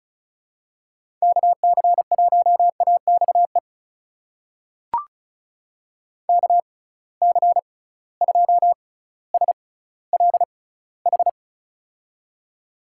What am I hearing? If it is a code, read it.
Morse code, heard as a keyed tone, about 36 wpm, spaced as for 18 wpm, 700 Hz. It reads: KC1AXE KC2SLH
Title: USA Call Signs - 6 Characters - 6x Character Spacing - 35wpm